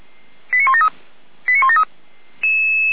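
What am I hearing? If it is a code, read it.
Five-tone selective call (ZVEI), heard as quick runs of five tones, tones of about 70 ms. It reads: 89173 89173
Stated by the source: ZVEI five-tone call received on SDR, Feuerwehr Orlamünde.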